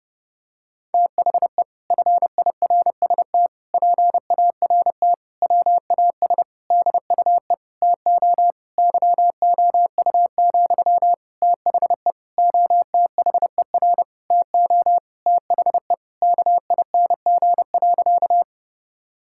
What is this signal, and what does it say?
Morse code, heard as a keyed tone, about 30 wpm, 700 Hz. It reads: THE FIRST PART WAS DUE TO YOU, THE OTHER TO THE KING.
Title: Sentences from Top 1000 Words - Review Part 6 - 30wpm